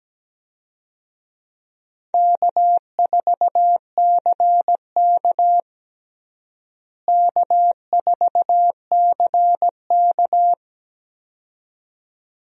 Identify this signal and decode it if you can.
Morse code, heard as a keyed tone, about 17 wpm, 700 Hz. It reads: K4CK K4CK
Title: USA Call Signs - 4 Characters - Repeated 2x - 17wpm